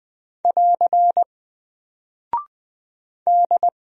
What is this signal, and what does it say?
Morse code, heard as a keyed tone, about 20 wpm, 700 Hz. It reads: <AR> D
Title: Single Character (Letter:Number:Punctuation:Prosign) 20wpm